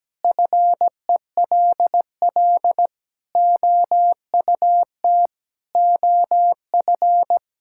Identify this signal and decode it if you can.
Morse code, heard as a keyed tone, about 17 wpm, 700 Hz. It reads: FELL OUT OF
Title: Sentences from Top 1600 Words 17wpm